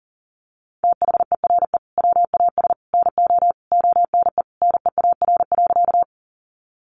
Morse code, heard as a keyed tone, about 40 wpm, 700 Hz. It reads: THERE WAS NO ONE NEAR.
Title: Sentences from Top 200 Words 40wpm